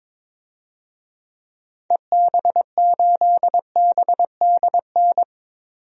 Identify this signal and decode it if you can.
Morse code, heard as a keyed tone, about 21 wpm, 700 Hz. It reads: EB8BDN
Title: DXCC-100 Call Signs - 6 Characters - 22wpm